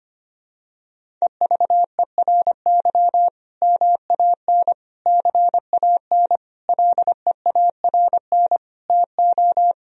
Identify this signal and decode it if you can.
Morse code, heard as a keyed tone, about 25 wpm, 700 Hz. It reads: EVERY MAN CAN LEARN TO
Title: Sentences from Top 1000 Words - Review Part 1 - 25wpm